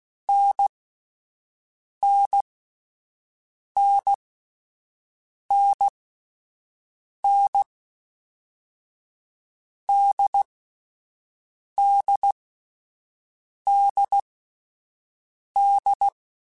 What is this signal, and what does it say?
Morse code, heard as a keyed tone, about 16 wpm, spaced as for 5 wpm, 800 Hz. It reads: NNNNN DDDD
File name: Difficult Combos / Difficult Combos-mp3s / NDB NDB6 @5